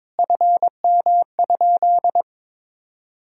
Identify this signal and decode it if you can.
Morse code, heard as a keyed tone, about 22 wpm, 700 Hz. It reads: FM?
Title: Sentences from Top 300 Words and Abbreviations 22wpm